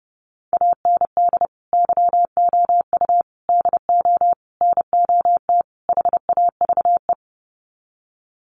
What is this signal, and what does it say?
Morse code, heard as a keyed tone, about 30 wpm, 700 Hz. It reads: AND YOU DO NOT HAVE
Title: Sets of 5 Words Encore 30wpm